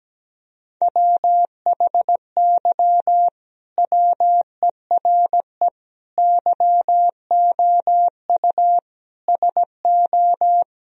Morse code, heard as a keyed tone, about 17 wpm, 700 Hz. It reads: WHY WERE YOU SO
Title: Sentences from Top 1000 Words - Review Part 4 - 17wpm